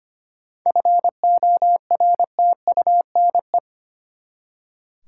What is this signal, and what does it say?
Morse code, heard as a keyed tone, about 25 wpm, 700 Hz. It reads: FORTUNE
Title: Seven Letter Words - 25wpm